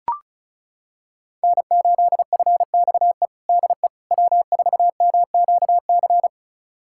Morse code, heard as a keyed tone, about 34 wpm, 700 Hz. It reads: N8FXE DE W4MQC
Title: USA Callsign DE Callsign 35wpm